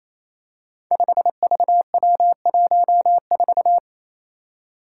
Morse code, heard as a keyed tone, about 28 wpm, 700 Hz. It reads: 5VW14
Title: Five Challenging Letter-Number Combinations 28wpm